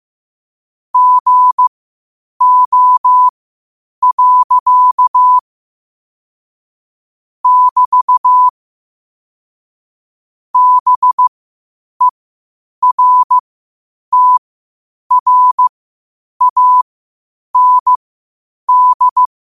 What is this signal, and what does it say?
Morse code, heard as a keyed tone, about 15 wpm, spaced as for 8 wpm, 1000 Hz. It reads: GO. <BT> BERTRAND
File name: qotdfiles / 8wpm